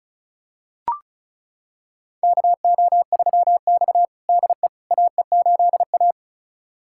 Morse code, heard as a keyed tone, about 36 wpm, 700 Hz. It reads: KO3X DE AE8A
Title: USA Callsign DE Callsign - 4 Characters - 35wpm